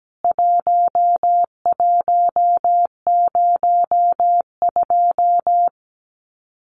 Morse code, heard as a keyed tone, about 17 wpm, 700 Hz. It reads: 1102
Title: CWT Exchange 17wpm